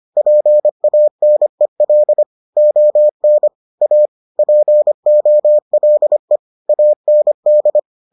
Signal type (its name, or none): Morse code